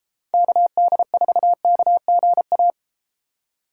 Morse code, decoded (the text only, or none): KD4KGA